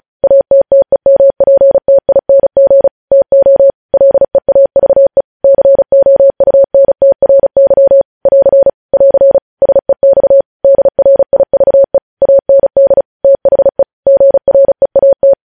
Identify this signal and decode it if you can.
Morse code, heard as a keyed tone, about 36 wpm, 550 Hz. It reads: ATTEMPTING TO LEAVE COUNTRY <AR> <AR> SEX DRIVE AND THE GREAT